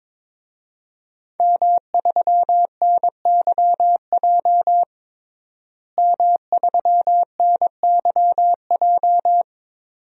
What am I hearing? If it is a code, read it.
Morse code, heard as a keyed tone, about 21 wpm, 700 Hz. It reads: M3NYJ M3NYJ
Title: DXCC-100 Call Signs - 5 Characters - Repeated 2x - 22wpm